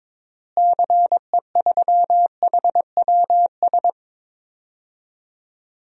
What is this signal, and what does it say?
Morse code, heard as a keyed tone, about 22 wpm, 700 Hz. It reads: CE3HWS
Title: DXCC-100 Call Signs - 6 Characters - Repeated 3x - 22wpm